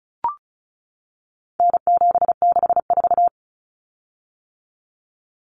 Morse code, heard as a keyed tone, about 34 wpm, 700 Hz. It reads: N764
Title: Common Cut Numbers - 4-Digit 35wpm